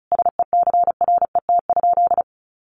Morse code, heard as a keyed tone, about 34 wpm, 700 Hz. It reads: SECRET?
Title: Sentences from Top 1200 Words 35wpm